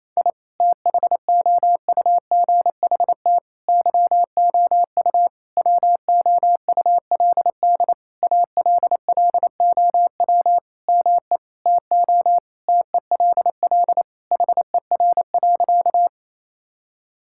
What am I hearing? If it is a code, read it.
Morse code, heard as a keyed tone, about 27 wpm, 700 Hz. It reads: I THOUGHT YOU WOULD ALLOW ME TO TELL HER.
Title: Sentences from Top 1000 Words - Review Part 5 - 28wpm